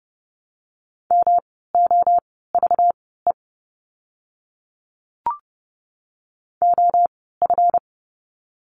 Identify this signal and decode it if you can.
Morse code, heard as a keyed tone, about 30 wpm, spaced as for 17 wpm, 700 Hz. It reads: MOVE OF
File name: course / 30:10wpm / 237 Lesson 38f - Letter q - Words 30wpm